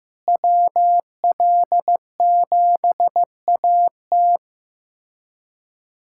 Morse code, heard as a keyed tone, about 15 wpm, 700 Hz. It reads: WL7AT
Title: USA Callsign DE Callsign 15wpm